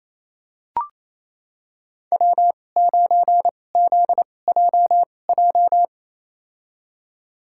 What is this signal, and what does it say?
Morse code, heard as a keyed tone, about 28 wpm, spaced as for 20 wpm, 700 Hz. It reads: W9ZJJ
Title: USA Call Signs - 5 Characters - 2x Character Spacing - 28wpm